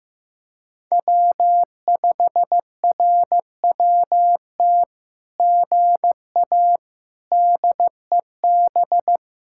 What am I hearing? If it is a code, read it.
Morse code, heard as a keyed tone, about 14 wpm, 700 Hz. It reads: W5RWT GA DEB
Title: SOTA QSO Practice 15wpm